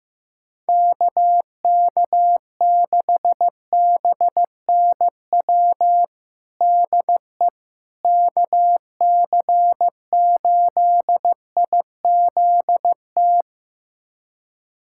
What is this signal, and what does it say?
Morse code, heard as a keyed tone, about 15 wpm, 700 Hz. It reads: KK6BNW DE KC8IZT
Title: USA Callsign DE Callsign 15wpm